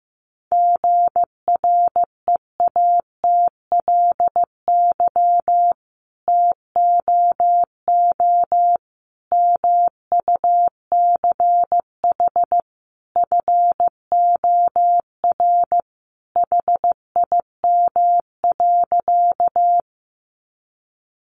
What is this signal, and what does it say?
Morse code, heard as a keyed tone, about 15 wpm, 700 Hz. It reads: GREATLY TOO MUCH FOR HIM.
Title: Sentences from Top 1400 Words 15wpm